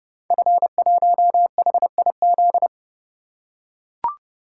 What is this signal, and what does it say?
Morse code, heard as a keyed tone, about 29 wpm, 700 Hz. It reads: F1HIZ
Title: Code Groups - 5 Letter:Numbers - 30wpm